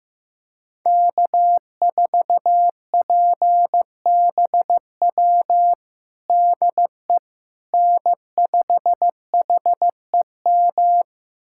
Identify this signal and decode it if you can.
Morse code, heard as a keyed tone, about 15 wpm, 700 Hz. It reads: K4PBW DE N5HEM
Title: USA Callsign DE Callsign - 5 Characters - 15wpm